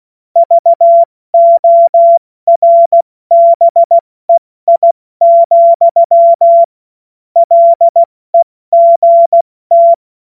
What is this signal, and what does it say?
Morse code, heard as a keyed tone, about 16 wpm, 680 Hz. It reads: VORBEI, LEGT
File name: Morsekurs / Witze / Witz 15. November 2022 70 F80